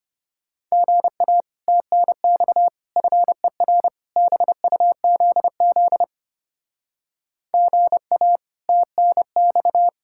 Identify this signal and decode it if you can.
Morse code, heard as a keyed tone, about 30 wpm, 700 Hz. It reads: GA TNX FER BUZZ GA TNX
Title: Speed-Racing Common QSO Phrases - 25wpm